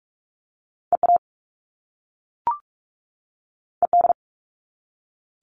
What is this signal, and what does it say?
Morse code, heard as a keyed tone, about 46 wpm, 700 Hz. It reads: EA ED
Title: All Cut Numbers - 2-Digit 45wpm